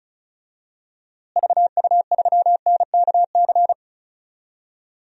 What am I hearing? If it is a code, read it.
Morse code, heard as a keyed tone, about 34 wpm, 700 Hz. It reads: VU3NKC
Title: DXCC-100 Call Signs - 6 Characters - Repeated 3x - 35wpm